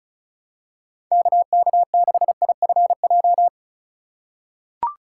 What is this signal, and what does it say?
Morse code, heard as a keyed tone, about 34 wpm, 700 Hz. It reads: KK6IFJ
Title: USA Call Signs - 3x Character Spacing - 35wpm